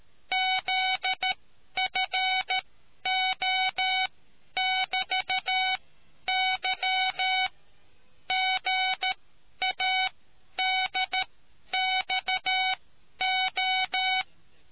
Morse code, heard as a keyed tone, about 13 wpm, spaced as for 10 wpm, 750 Hz. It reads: ZFO<BT>Y GADXO